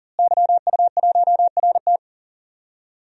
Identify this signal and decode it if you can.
Morse code, heard as a keyed tone, about 40 wpm, 700 Hz. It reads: YU1RT
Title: DXCC-100 Call Signs - 5 Characters - Repeated 3x - 40wpm